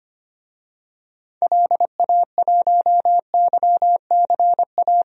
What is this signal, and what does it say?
Morse code, heard as a keyed tone, about 25 wpm, 700 Hz. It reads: LA1YCA
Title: DXCC-100 Call Signs - 6 Characters - 25wpm